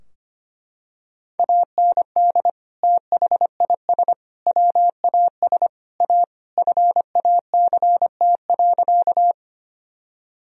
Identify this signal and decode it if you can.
Morse code, heard as a keyed tone, about 25 wpm, 700 Hz. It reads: AND THIS WAS A FACT.
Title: Sentences from Top 1000 Words - Review Part 3 - 25wpm